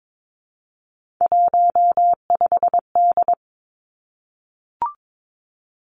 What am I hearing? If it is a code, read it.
Morse code, heard as a keyed tone, about 22 wpm, 700 Hz. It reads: 15D
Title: Three Challenging Letter-Number Combinations 22wpm